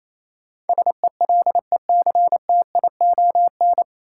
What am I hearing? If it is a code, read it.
Morse code, heard as a keyed tone, about 28 wpm, 700 Hz. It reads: SELECTION
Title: Sentences from Top 1400 Words - New Word Drill - 28wpm